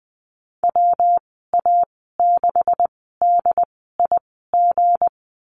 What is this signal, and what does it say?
Morse code, heard as a keyed tone, about 20 wpm, spaced as for 14 wpm, 700 Hz. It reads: WA6DIG
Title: 213 Lesson 34g - Number 8 - Call Signs 20wpm